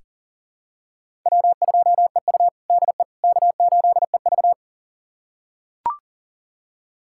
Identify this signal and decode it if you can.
Morse code, heard as a keyed tone, about 40 wpm, 700 Hz. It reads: W2EU DE K8EV
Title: USA Callsign DE Callsign - 4 Characters - 40wpm